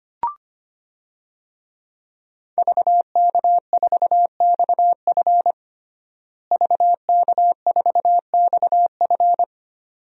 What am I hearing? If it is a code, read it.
Morse code, heard as a keyed tone, about 25 wpm, 700 Hz. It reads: VK4XF VK4XF
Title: DXCC-100 Call Signs - 5 Characters - Repeated 2x - 25wpm